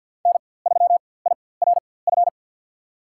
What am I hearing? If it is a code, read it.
Morse code, heard as a keyed tone, about 47 wpm, spaced as for 23 wpm, 700 Hz. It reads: N3IRF
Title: USA Call Signs - 4x Character Spacing - 50wpm